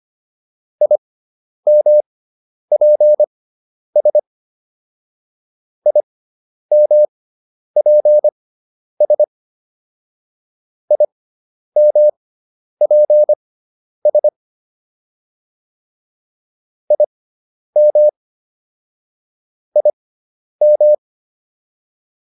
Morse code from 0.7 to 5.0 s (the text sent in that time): IMPS